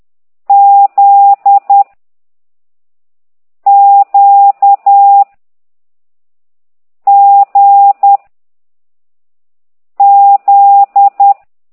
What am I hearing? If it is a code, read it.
Morse code, heard as a keyed tone, about 10 wpm, 800 Hz. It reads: Z Q G Z